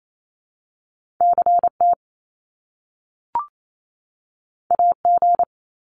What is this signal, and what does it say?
Morse code, heard as a keyed tone, about 28 wpm, 700 Hz. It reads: CT AG
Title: Top Two Letter Combinations in Words 28wpm